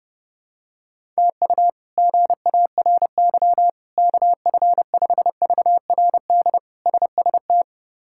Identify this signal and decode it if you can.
Morse code, heard as a keyed tone, about 32 wpm, 700 Hz. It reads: TU GARY KF5VRD SST
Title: K1USN SST QSO Practice - 30wpm